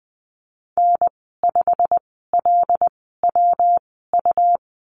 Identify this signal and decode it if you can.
Morse code, heard as a keyed tone, about 20 wpm, spaced as for 15 wpm, 700 Hz. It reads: N5LWU
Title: 173 Lesson 28g - Number 9 - Call Signs 20wpm